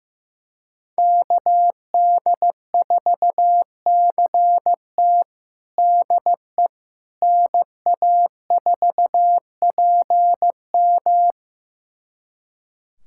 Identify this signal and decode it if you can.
Morse code, heard as a keyed tone, about 15 wpm, 700 Hz. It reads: KD4CT DE NA4PM
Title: USA Callsign DE Callsign - 5 Characters - 15wpm